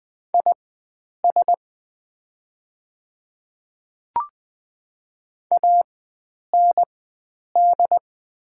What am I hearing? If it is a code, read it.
Morse code, heard as a keyed tone, about 20 wpm, spaced as for 9 wpm, 700 Hz. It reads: IS AND
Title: 4x Character Spacing - Top 100 Words - 20wpm